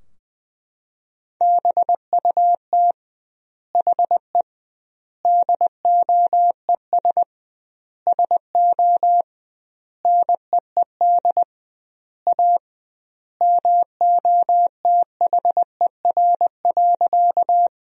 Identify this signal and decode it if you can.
Morse code, heard as a keyed tone, about 20 wpm, 700 Hz. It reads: BUT HE DOES SO NEED A MOTHER.
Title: 2x Word-Spacing - Sentences from Top 200 Words - 20wpm